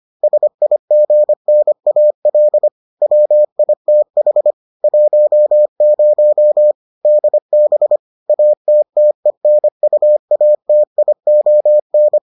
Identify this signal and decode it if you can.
Morse code, heard as a keyed tone, about 25 wpm, 600 Hz. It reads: SIGNAL WITH 10 DB ATTENUATION